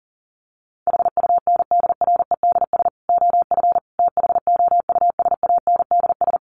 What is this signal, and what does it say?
Morse code, heard as a keyed tone, about 37 wpm, 700 Hz. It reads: HUNDREDS OF THOUSANDS
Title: Sets of 5 Words Encore 40wpm